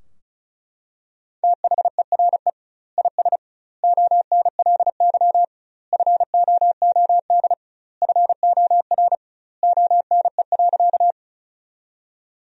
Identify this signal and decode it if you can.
Morse code, heard as a keyed tone, about 34 wpm, 700 Hz. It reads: THERE IS ONLY FOOD FOR ONE.
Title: 2x Word-Spacing - Sentences from Top 300 Words - 35wpm